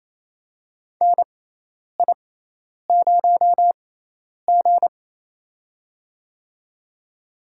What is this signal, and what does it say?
Morse code, heard as a keyed tone, about 28 wpm, spaced as for 10 wpm, 700 Hz. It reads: NI0G
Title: USA Call Signs - 6x Character Spacing - 28wpm